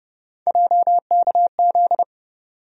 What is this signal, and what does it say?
Morse code, heard as a keyed tone, about 30 wpm, 700 Hz. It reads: JKZ
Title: USA Call Sign Postfixes - 3 Letters 30wpm